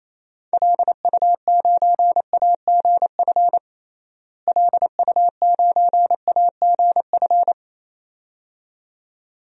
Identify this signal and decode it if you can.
Morse code, heard as a keyed tone, about 27 wpm, 700 Hz. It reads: LU9AGF LU9AGF
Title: DXCC-100 Call Signs - 6 Characters - Repeated 2x - 28wpm